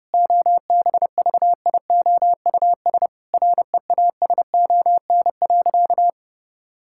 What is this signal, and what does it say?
Morse code, heard as a keyed tone, about 30 wpm, 700 Hz. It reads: OBVIOUS REASON.
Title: Sentences from Top 1600 Words 30wpm